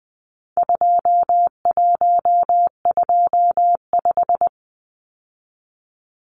Morse code, heard as a keyed tone, about 20 wpm, 700 Hz. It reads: 2125
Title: Numbers - 4-Digit 20wpm